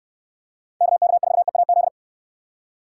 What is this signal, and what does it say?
Morse code, heard as a keyed tone, about 69 wpm, 700 Hz. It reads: KK4EAZ